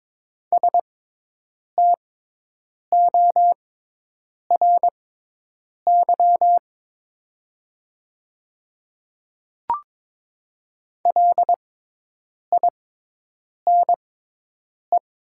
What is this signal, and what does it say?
Morse code, heard as a keyed tone, about 22 wpm, spaced as for 8 wpm, 700 Hz. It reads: STORY LINE
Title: 6x Character Spacing - Top 300 Words - 22wpm